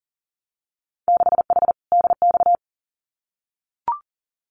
Morse code, heard as a keyed tone, about 40 wpm, 700 Hz. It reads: 6H DX
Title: Winter Field Day Exchanges Encore 40wpm